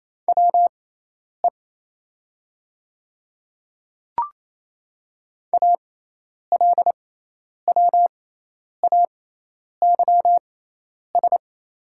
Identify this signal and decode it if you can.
Morse code, heard as a keyed tone, about 28 wpm, spaced as for 10 wpm, 690 Hz. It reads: WE ALWAYS
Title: 6x Character Spacing - Top 500 Words - 28wpm